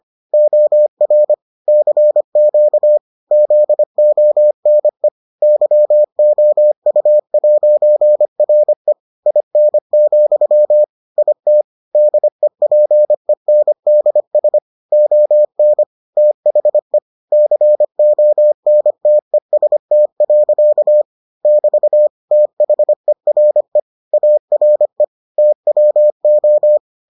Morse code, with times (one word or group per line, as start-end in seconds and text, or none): 0.3-1.3 OR
1.7-3.0 CQ
3.3-5.1 ZONE
5.4-8.9 YOU'RE
9.3-10.8 IN,
11.2-11.6 IT
11.9-14.6 DEPENDS
14.9-15.8 ON
16.2-17.0 THE
17.3-21.0 CONTEST.
21.4-22.1 <BT>
22.3-23.8 THERE
24.1-25.0 ARE
25.4-26.8 TWO